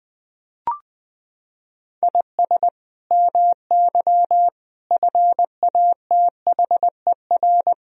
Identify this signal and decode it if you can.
Morse code, heard as a keyed tone, about 21 wpm, 700 Hz. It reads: IS MY FATHER
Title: Sentences from Top 300 Words and Abbreviations 20wpm